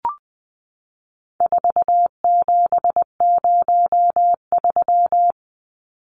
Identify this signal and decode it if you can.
Morse code, heard as a keyed tone, about 20 wpm, 700 Hz. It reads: 4703
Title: Numbers - 4-Digit 20wpm